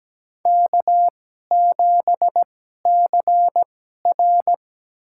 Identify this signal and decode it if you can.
Morse code, heard as a keyed tone, about 17 wpm, spaced as for 12 wpm, 700 Hz. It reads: K7CR
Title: USA Call Signs - 4 Characters - 2x Character Spacing - 17wpm